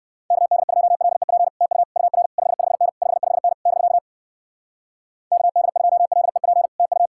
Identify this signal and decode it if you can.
Morse code, heard as a keyed tone, about 59 wpm, 700 Hz. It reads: KD2LEP TU UR 55N 55N <BK> KD2LEP TU